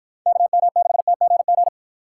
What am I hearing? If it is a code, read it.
Morse code, heard as a keyed tone, about 52 wpm, 700 Hz. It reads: KM6TGG